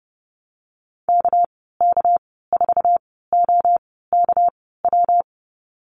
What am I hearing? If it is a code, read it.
Morse code, heard as a keyed tone, about 30 wpm, spaced as for 17 wpm, 700 Hz. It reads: KK4OKW
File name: Lesson 37g - Letter x - Call Signs 30wpm